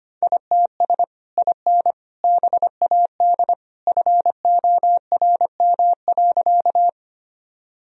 Morse code, heard as a keyed tone, about 25 wpm, 700 Hz. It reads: ITS IN BAD FORM.
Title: Sentences from Top 1000 Words - Review Part 1 - 25wpm